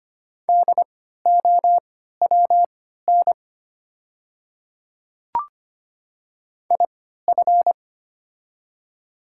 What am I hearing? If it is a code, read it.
Morse code, heard as a keyed tone, about 25 wpm, spaced as for 14 wpm, 690 Hz. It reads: DOWN IF
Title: 3x Character Spacing - Top 100 Words - 25wpm